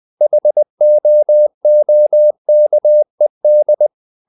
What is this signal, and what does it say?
Morse code, heard as a keyed tone, about 20 wpm, 600 Hz. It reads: HOOKED
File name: the old man and the sea / the old man and the sea part 1 20wpm